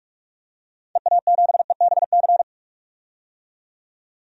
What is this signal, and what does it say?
Morse code, heard as a keyed tone, about 46 wpm, 700 Hz. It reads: EA7EBC
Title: DXCC-100 Call Signs - 6 Characters - 45wpm